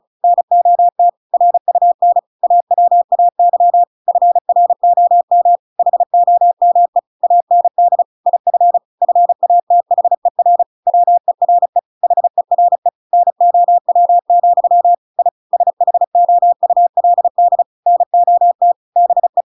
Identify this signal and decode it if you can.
Morse code, heard as a keyed tone, about 34 wpm, 700 Hz. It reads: NOT RUN AWAY FROM HOME AND IF FATHER WERE HERE NOW, I SHOULD NOT BE